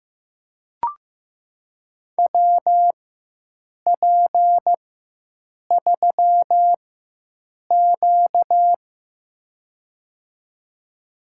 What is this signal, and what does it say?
Morse code, heard as a keyed tone, about 15 wpm, spaced as for 7 wpm, 700 Hz. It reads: WP3Q
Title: USA Call Signs - 4 Characters - 4x Character Spacing - 15wpm